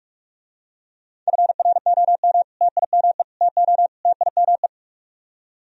Morse code, heard as a keyed tone, about 44 wpm, 700 Hz. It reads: FROM TIME TO TIME